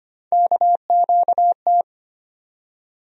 Morse code, heard as a keyed tone, about 25 wpm, 700 Hz. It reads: KQT
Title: USA Call Sign Postfixes - 3 Letters 25wpm